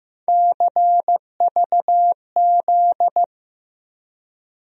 Morse code, heard as a keyed tone, about 15 wpm, 700 Hz. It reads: CVZ
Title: USA Call Sign Postfixes - 3 Letters 15wpm